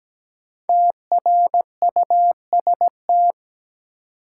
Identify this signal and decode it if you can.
Morse code, heard as a keyed tone, about 17 wpm, 700 Hz. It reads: TRUST
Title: Five Letter Words - 17wpm